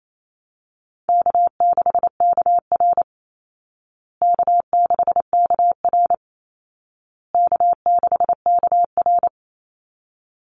Morse code, heard as a keyed tone, about 28 wpm, 700 Hz. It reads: K6KR K6KR K6KR
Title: Easy Call Signs Repeated 3x - 28wpm